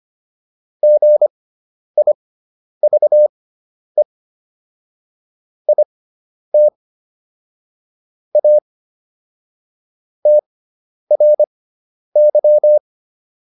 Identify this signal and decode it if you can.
Morse code, heard as a keyed tone, about 25 wpm, spaced as for 10 wpm, 600 Hz. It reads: GIVE IT A TRY